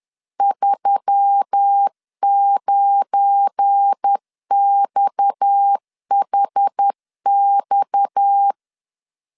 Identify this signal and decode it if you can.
Morse code, heard as a keyed tone, about 10 wpm, 800 Hz. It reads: 39XHX
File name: LEZIONI / 9 2 10wpm 2736